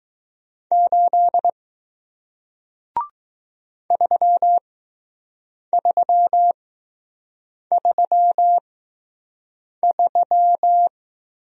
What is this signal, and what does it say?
Morse code, heard as a keyed tone, about 20 wpm, spaced as for 6 wpm, 700 Hz. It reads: 8 3333